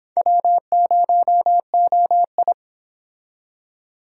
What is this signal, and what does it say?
Morse code, heard as a keyed tone, about 25 wpm, 700 Hz. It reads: W0OI